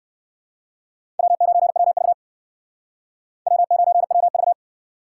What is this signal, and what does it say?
Morse code, heard as a keyed tone, about 56 wpm, 700 Hz. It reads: W9WV W9WV